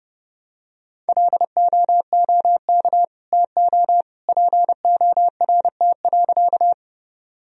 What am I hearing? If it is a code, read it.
Morse code, heard as a keyed tone, about 30 wpm, 700 Hz. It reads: LOOK TO PORT.
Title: Sentences from Top 1000 Words - Review Part 1 - 30wpm